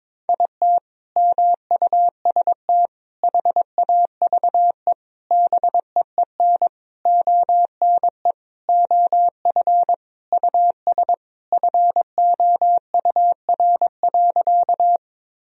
Morse code, heard as a keyed tone, about 22 wpm, 700 Hz. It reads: IT MUST HAVE BEEN ONE OF US FOUR.
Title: Sentences from Top 1000 Words - Review Part 1 - 22wpm